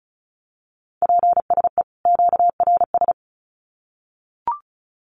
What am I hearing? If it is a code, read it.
Morse code, heard as a keyed tone, about 34 wpm, 700 Hz. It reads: PSE QRS
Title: Common QSO Phrases 35wpm